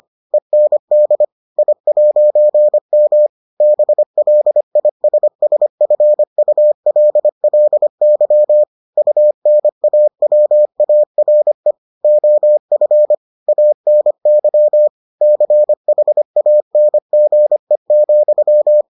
Morse code, heard as a keyed tone, about 25 wpm, 600 Hz. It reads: END I'M BLISSFULLY UNAWARE OF ANY CHANGE,